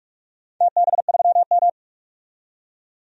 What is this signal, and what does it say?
Morse code, heard as a keyed tone, about 46 wpm, 700 Hz. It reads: TB3M
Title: DXCC-100 Call Signs - 4 Characters - 45wpm